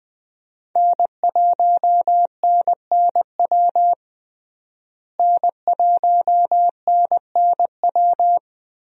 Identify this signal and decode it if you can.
Morse code, heard as a keyed tone, about 21 wpm, 700 Hz. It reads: N1NNW N1NNW
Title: USA Call Signs - 5 Characters - Repeated 3x - 20wpm